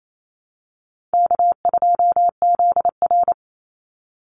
Morse code, heard as a keyed tone, about 28 wpm, 700 Hz. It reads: K2ZR
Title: Easy Call Signs Repeated 2x - 28wpm